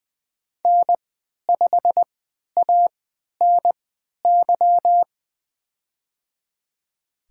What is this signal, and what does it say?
Morse code, heard as a keyed tone, about 20 wpm, spaced as for 11 wpm, 700 Hz. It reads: N5ANY
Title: USA Call Signs - 3x Character Spacing - 20wpm